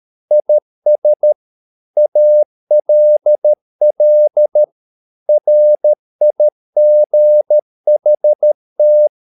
Morse code, heard as a keyed tone, about 13 wpm, 600 Hz. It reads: IS ALL RIGHT
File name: mp3 files / Lexiphiles - 13wpm